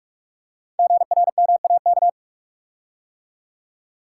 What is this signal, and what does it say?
Morse code, heard as a keyed tone, about 44 wpm, 690 Hz. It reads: GRMAK
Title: Code Groups - 5 Letters - 45wpm